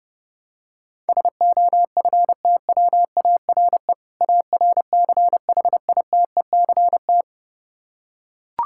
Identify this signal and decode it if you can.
Morse code, heard as a keyed tone, about 30 wpm, 700 Hz. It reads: SOFTWARE ARCHITECT
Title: Practice QSO - Exchange 3 - More Info - 30wpm